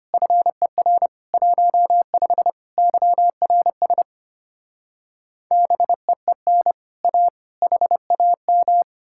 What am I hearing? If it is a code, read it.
Morse code, heard as a keyed tone, about 33 wpm, 700 Hz. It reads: FER 15 YRS BEEN A HAM